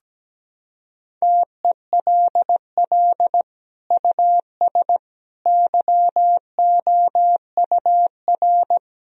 Morse code, heard as a keyed tone, about 17 wpm, 700 Hz. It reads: TELL US YOUR
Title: Sentences from Top 1000 Words - Review Part 4 - 17wpm